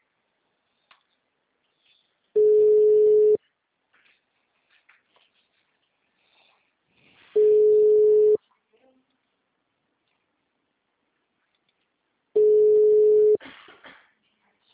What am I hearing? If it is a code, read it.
Ringback tone (Europe, 425 Hz), with speech.